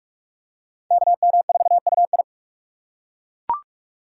Morse code, heard as a keyed tone, about 43 wpm, 700 Hz. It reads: KM4UI